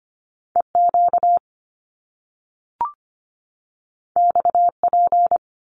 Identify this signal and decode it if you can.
Morse code, heard as a keyed tone, about 25 wpm, 700 Hz. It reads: EQ XP
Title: Top Two Letter Combinations in Words 25wpm